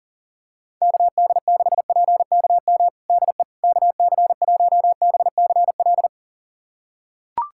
Morse code, heard as a keyed tone, about 40 wpm, 700 Hz. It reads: KD6PKM DE KC1BCL
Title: USA Callsign DE Callsign 40wpm